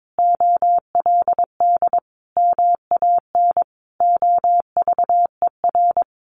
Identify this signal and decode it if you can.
Morse code, heard as a keyed tone, about 22 wpm, 700 Hz. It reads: OLD MAN OVER
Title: Sentences from Top 200 Words 22wpm